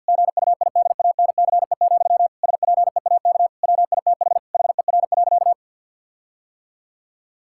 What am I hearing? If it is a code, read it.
Morse code, heard as a keyed tone, about 50 wpm, 700 Hz. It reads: GUIDANCE, SPEAK WITH HER.
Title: Sentences from Top 1600 Words 50wpm